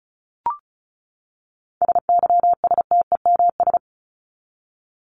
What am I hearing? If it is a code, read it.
Morse code, heard as a keyed tone, about 34 wpm, 700 Hz. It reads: SYSTEMS
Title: ES:S Suffix - Top 100 Words with Base Words 35wpm